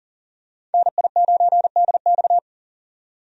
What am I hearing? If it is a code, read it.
Morse code, heard as a keyed tone, about 40 wpm, 700 Hz. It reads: NI9DX
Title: USA Call Signs - 5 Characters - 40wpm